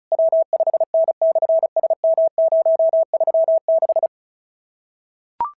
Morse code, heard as a keyed tone, about 34 wpm, 650 Hz. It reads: W5N/SM036
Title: SOTA QSO Practice 35wpm